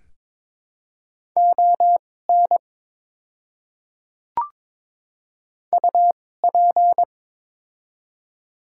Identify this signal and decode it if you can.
Morse code, heard as a keyed tone, about 22 wpm, spaced as for 16 wpm, 700 Hz. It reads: ON UP
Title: 2x Character Spacing - Top 100 Words - 22wpm